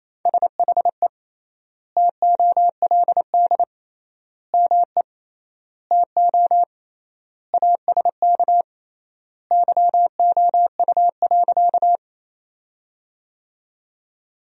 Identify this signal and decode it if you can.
Morse code, heard as a keyed tone, about 28 wpm, 700 Hz. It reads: SHE TOLD ME TO ASK YOU.
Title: 3x Word-Spacing - Sentences from Top 400 Words - 28wpm